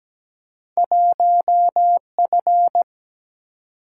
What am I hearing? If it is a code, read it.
Morse code, heard as a keyed tone, about 17 wpm, 700 Hz. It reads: 1F
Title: Single-Letter Single-Number 17wpm